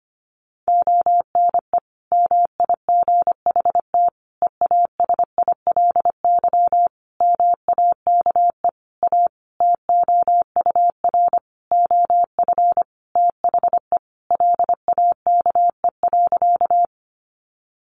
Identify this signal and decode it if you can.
Morse code, heard as a keyed tone, about 25 wpm, 700 Hz. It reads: ONE MIGHT EASILY MAKE A TOUR OF THE LAKE.
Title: Sentences from Top 1400 Words 25wpm